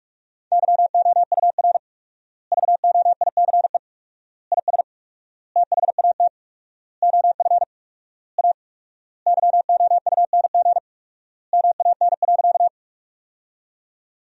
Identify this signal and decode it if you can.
Morse code, heard as a keyed tone, about 41 wpm, 700 Hz. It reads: YOUR VOICE IS THAT OF A YOUNG MAN.
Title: 4x Word-Spacing - Sentences from Top 500 Words - 45wpm